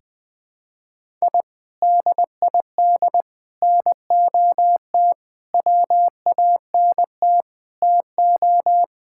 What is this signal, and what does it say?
Morse code, heard as a keyed tone, about 20 wpm, 700 Hz. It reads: I DID NOT WANT TO
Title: Sentences with 5-Word Phrases - 20wpm